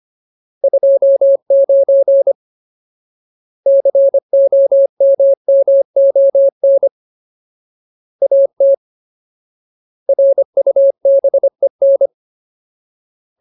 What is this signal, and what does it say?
Morse code, heard as a keyed tone, about 25 wpm, 550 Hz. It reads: 29 COMMON AT RUBEN
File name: Ultimate Word List / 25WPM Ultimate